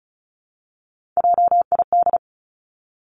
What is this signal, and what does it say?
Morse code, heard as a keyed tone, about 36 wpm, 700 Hz. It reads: JID
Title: Three Challenging Letter-Number Combinations 35wpm